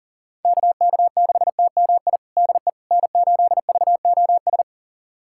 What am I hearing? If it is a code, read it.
Morse code, heard as a keyed tone, about 37 wpm, 700 Hz. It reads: KK6TMI DE N8VOS
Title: USA Callsign DE Callsign 40wpm